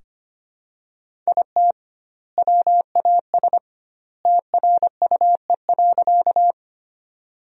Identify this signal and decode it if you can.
Morse code, heard as a keyed tone, about 25 wpm, 700 Hz. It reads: IT WAS TRUE.
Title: 2x Word-Spacing - Sentences from Top 400 Words - 25wpm